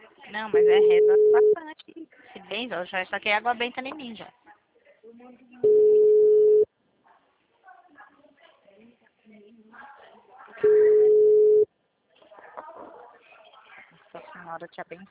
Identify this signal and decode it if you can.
Ringback tone (Europe, 425 Hz), with speech.
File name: out- -2069- - - .24279